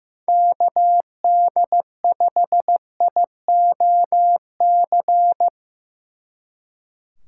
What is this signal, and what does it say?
Morse code, heard as a keyed tone, about 17 wpm, 700 Hz. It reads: KD5IOC
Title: Speed Racing - USA 2x3 Call Signs 15wpm